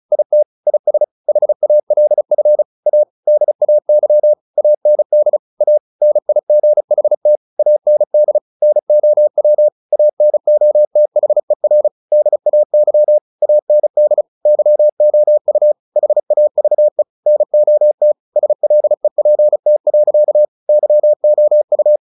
Morse code, heard as a keyed tone, about 34 wpm, 600 Hz. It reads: IT IS HALF A DAY AND A NIGHT AND NOW ANOTHER DAY AND YOU HAVE NOT SLEPT. YOU
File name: the old man and the sea 2 35wpm